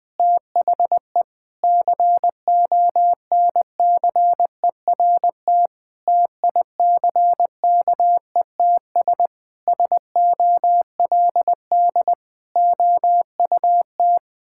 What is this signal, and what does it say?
Morse code, heard as a keyed tone, about 20 wpm, 700 Hz. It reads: THE CONCERT TICKETS SOLD OUT
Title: Sentences with 6-Word Phrases - Spoken Beginning - 20wpm